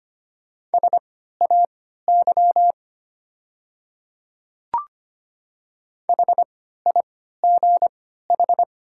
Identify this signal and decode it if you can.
Morse code, heard as a keyed tone, about 25 wpm, spaced as for 14 wpm, 700 Hz. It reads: SAY HIGH
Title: 3x Character Spacing - Top 300 Words - 25wpm